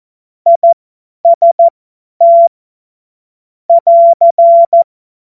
Morse code, heard as a keyed tone, about 14 wpm, spaced as for 10 wpm, 680 Hz. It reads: IST <AR>
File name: Morsekurs / Witze / Witz 30. Dezember 2025 50 F70